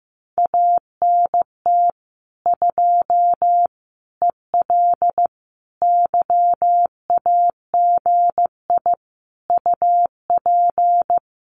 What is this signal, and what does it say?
Morse code, heard as a keyed tone, about 15 wpm, 700 Hz. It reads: ANT 2 EL YAGI UP
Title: Common QSO Phrases Encore 15wpm